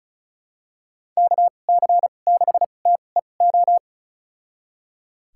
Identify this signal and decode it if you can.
Morse code, heard as a keyed tone, about 34 wpm, spaced as for 25 wpm, 700 Hz. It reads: KC6TEO
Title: USA Call Signs - 6 Characters - 2x Character Spacing - 35wpm